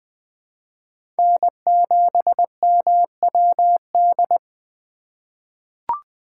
Morse code, heard as a keyed tone, about 20 wpm, 700 Hz. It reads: N7MWD
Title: USA Call Signs - 5 Characters - 3x Character Spacing - 20wpm